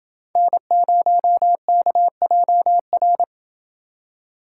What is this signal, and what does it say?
Morse code, heard as a keyed tone, about 28 wpm, 700 Hz. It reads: N0KJR